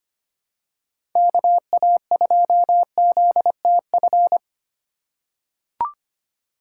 Morse code, heard as a keyed tone, about 25 wpm, 700 Hz. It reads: KA2ZTF
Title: USA Call Signs - 2x Character Spacing - 25wpm